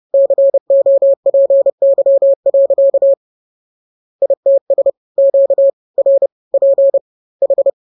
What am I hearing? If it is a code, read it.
Morse code, heard as a keyed tone, about 30 wpm, 550 Hz. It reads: COPY. ITS QRP H